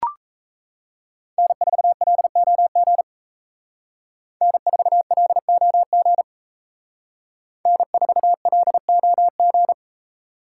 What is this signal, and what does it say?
Morse code, heard as a keyed tone, about 34 wpm, 700 Hz. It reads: N4LOG N4LOG N4LOG
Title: Speed Racing - USA 1x3 Call Signs 28wpm